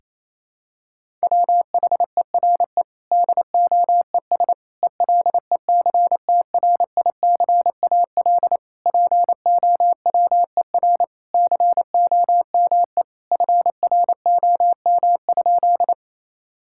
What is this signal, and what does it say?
Morse code, heard as a keyed tone, about 28 wpm, 700 Hz. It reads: WHERE DOES ELECTRICAL POWER COME FROM?
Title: Sentences from Top 1000 Words - Review Part 6 - 28wpm